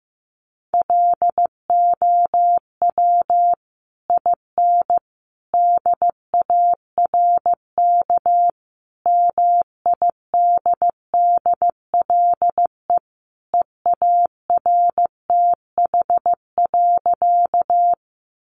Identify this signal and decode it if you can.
Morse code, heard as a keyed tone, about 15 wpm, 700 Hz. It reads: LOW IN DARK MIDDLE EARTH.
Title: Sentences from Top 900 Words 15wpm